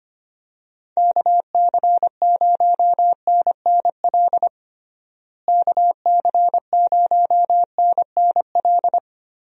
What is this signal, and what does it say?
Morse code, heard as a keyed tone, about 25 wpm, 700 Hz. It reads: KC0NNL KC0NNL
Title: USA Call Signs - 6 Characters - Repeated 3x - 25wpm